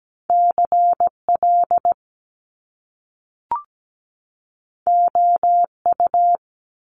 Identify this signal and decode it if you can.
Morse code, heard as a keyed tone, about 17 wpm, 700 Hz. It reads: CL OU
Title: Top Two Letter Combinations in Words 17wpm